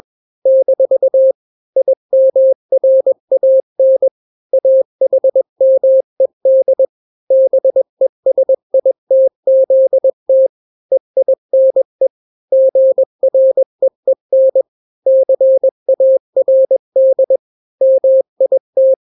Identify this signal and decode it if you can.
Morse code, heard as a keyed tone, about 21 wpm, 530 Hz. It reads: - IMRAN AHMED BESITZT EINE GREEN CARD MIT
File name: daily / 20wpm